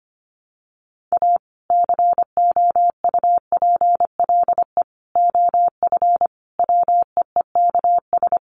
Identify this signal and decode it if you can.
Morse code, heard as a keyed tone, about 25 wpm, 700 Hz. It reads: A COUPLE OF WEEKS
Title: Sets of 5 Words 25wpm